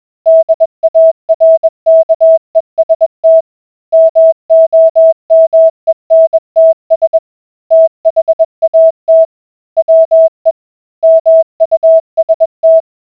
Morse code, heard as a keyed tone, about 21 wpm, 650 Hz. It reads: DARKEST MOMENTS THAT WE MUST